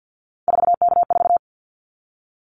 Morse code, heard as a keyed tone, about 50 wpm, 700 Hz. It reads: 4K4
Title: Callsign Prefixes for Top 100 DXCC - 50wpm